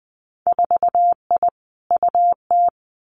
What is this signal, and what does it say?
Morse code, heard as a keyed tone, about 20 wpm, 700 Hz. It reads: 4I UT
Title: Winter Field Day Exchanges Encore 20wpm